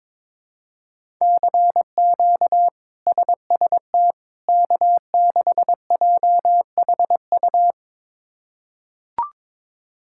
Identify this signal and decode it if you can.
Morse code, heard as a keyed tone, about 22 wpm, 700 Hz. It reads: CQ SST K6JHU
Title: K1USN SST QSO Practice - 22wpm